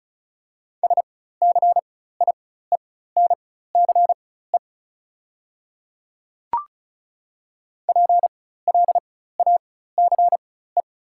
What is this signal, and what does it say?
Morse code, heard as a keyed tone, about 34 wpm, spaced as for 16 wpm, 690 Hz. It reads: SCIENCE PLACE
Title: 4x Character Spacing - Top 500 Words - 35wpm